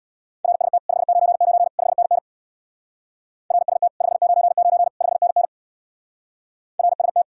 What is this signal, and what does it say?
Morse code, heard as a keyed tone, about 66 wpm, 700 Hz. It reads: RST 599 5NN RST 599 5NN RST